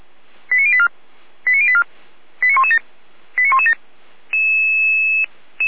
Five-tone selective call (ZVEI), heard as quick runs of five tones, tones of about 70 ms. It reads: 89084 89084 89117 89117
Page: SDR/ZVEI